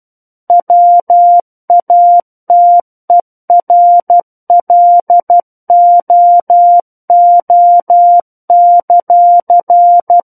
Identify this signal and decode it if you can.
Morse code, heard as a keyed tone, about 12 wpm, 700 Hz. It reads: WATERLOO;
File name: Book One / 03-Chapter 3 - 12 WPM